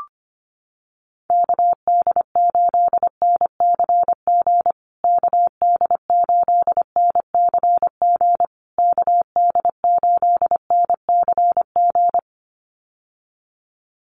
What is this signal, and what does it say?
Morse code, heard as a keyed tone, about 25 wpm, 700 Hz. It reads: KD8NCG KD8NCG KD8NCG
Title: USA Call Signs Repeated 3x 25wpm